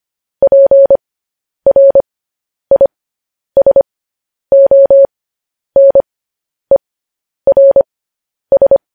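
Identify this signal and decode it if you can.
Morse code, heard as a keyed tone, about 25 wpm, spaced as for 10 wpm, 550 Hz. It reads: PRISONERS
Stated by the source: Morse transmission of news headlines.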